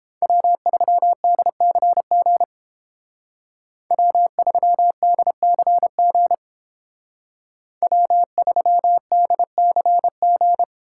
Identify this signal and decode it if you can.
Morse code, heard as a keyed tone, about 30 wpm, 700 Hz. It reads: W3DCG W3DCG W3DCG